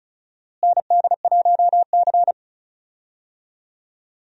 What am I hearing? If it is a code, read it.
Morse code, heard as a keyed tone, about 34 wpm, 700 Hz. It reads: ND1C
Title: USA Call Signs - 4 Characters - Repeated 3x - 35wpm